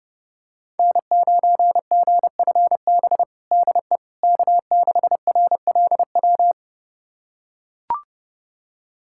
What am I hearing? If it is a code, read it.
Morse code, heard as a keyed tone, about 30 wpm, 700 Hz. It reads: N9GFB DE K6RLW
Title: USA Callsign DE Callsign - 5 Characters - 30wpm